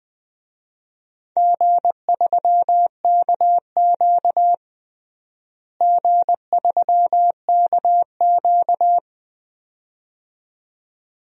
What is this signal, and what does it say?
Morse code, heard as a keyed tone, about 20 wpm, 700 Hz. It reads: G3KQ G3KQ
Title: DXCC-100 Call Signs - 4 Characters - Repeated 2x - 20wpm